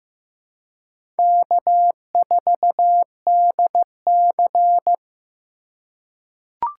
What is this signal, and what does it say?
Morse code, heard as a keyed tone, about 15 wpm, 700 Hz. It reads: K4DC
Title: USA Call Signs - 2x Character Spacing - 15wpm